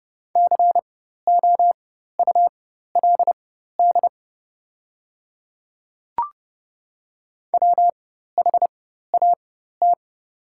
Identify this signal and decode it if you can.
Morse code, heard as a keyed tone, about 31 wpm, spaced as for 14 wpm, 700 Hz. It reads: COULD WHAT
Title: 4x Character Spacing - Top 100 Words - 30wpm